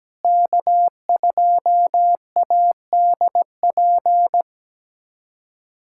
Morse code, heard as a keyed tone, about 17 wpm, 700 Hz. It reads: K2ADP
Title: USA Callsign DE Callsign - 5 Characters - 17wpm